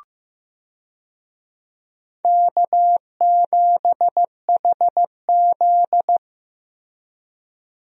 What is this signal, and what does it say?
Morse code, heard as a keyed tone, about 15 wpm, 700 Hz. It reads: K7HZ
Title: USA Call Signs - 4 Characters - Repeated 3x - 15wpm